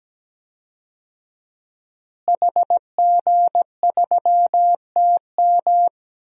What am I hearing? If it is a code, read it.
Morse code, heard as a keyed tone, about 17 wpm, 700 Hz. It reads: HG3TM
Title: DXCC-100 Call Signs - 5 Characters - Repeated 2x - 17wpm